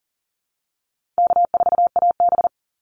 Morse code, heard as a keyed tone, about 40 wpm, 700 Hz. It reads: K4AB
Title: Easy Call Signs - 40wpm